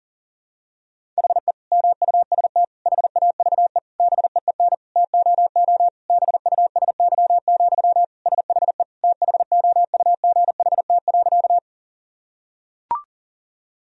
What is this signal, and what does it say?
Morse code, heard as a keyed tone, about 37 wpm, 700 Hz. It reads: HE MUST HAVE BEEN TOO BUSY, SHE THOUGHT.
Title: Sentences from Top 1000 Words - Review Part 3 - 40wpm